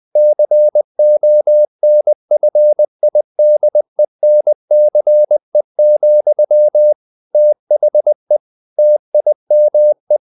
Morse code, heard as a keyed tone, about 20 wpm, 600 Hz. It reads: CONFIDENCE, THE TIME
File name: the old man and the sea / the old man and the sea part 1 20wpm